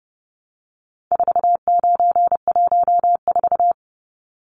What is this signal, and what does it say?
Morse code, heard as a keyed tone, about 30 wpm, 700 Hz. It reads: 4914
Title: Numbers - 4-Digit 30wpm